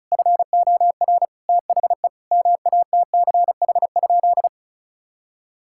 Morse code, heard as a keyed tone, about 34 wpm, 700 Hz. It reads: FOR THE MATCH?
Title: Sentences from Top 1600 Words 35wpm